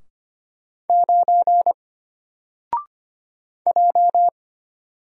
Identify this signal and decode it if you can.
Morse code, heard as a keyed tone, about 25 wpm, 700 Hz. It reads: 9 J
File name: Single-Letter-Number-25wpm